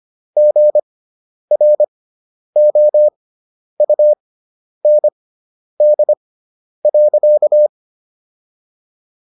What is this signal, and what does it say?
Morse code, heard as a keyed tone, about 25 wpm, spaced as for 10 wpm, 600 Hz. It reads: GROUND.